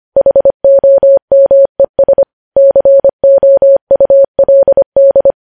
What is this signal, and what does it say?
Morse code, heard as a keyed tone, about 25 wpm, 550 Hz. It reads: HOMES COULD
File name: podcast / News headlines-1x-25x25